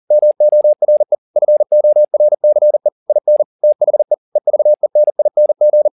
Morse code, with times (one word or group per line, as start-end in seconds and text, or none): 0.1-1.1 MORE
1.4-2.9 FORCE
3.1-3.4 IN
3.6-4.1 THE
4.4-5.9 EVENING